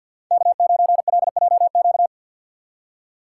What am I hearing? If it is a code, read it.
Morse code, heard as a keyed tone, about 47 wpm, 700 Hz. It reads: K9LJQ
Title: USA Call Signs - 5 Characters - 4x Character Spacing - 50wpm